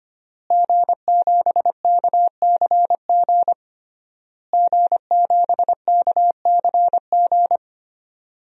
Morse code, heard as a keyed tone, about 25 wpm, 700 Hz. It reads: G7KCG G7KCG
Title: DXCC-100 Call Signs - 5 Characters - Repeated 3x - 25wpm